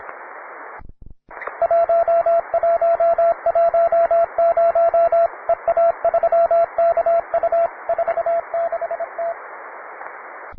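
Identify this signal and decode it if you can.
Morse code, heard as a keyed tone, about 25 wpm, 670 Hz, with radio noise. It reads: 1110 EA3KU 46T
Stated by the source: Morse code QTC list traffic from the Worked All Europe (WAE) contest, advanced level.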